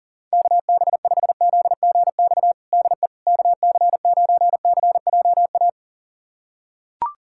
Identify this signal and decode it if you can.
Morse code, heard as a keyed tone, about 37 wpm, 700 Hz. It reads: KB5ZGX DE KC9CJA
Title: USA Callsign DE Callsign - 6 Characters - 40wpm